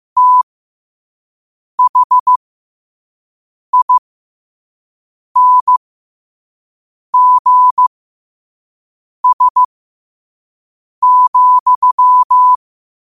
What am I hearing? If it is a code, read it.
Morse code, heard as a keyed tone, about 15 wpm, spaced as for 5 wpm, 1000 Hz. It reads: THINGS,
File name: qotdfiles / 5wpm